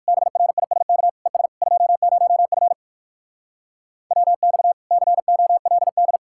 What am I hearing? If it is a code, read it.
Morse code, heard as a keyed tone, about 58 wpm, 700 Hz. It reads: BRISK ES 20F WX COLD